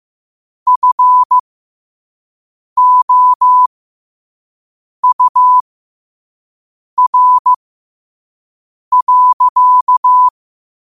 Morse code, heard as a keyed tone, about 15 wpm, spaced as for 5 wpm, 1000 Hz. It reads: FOUR.
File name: qotdfiles / 5wpm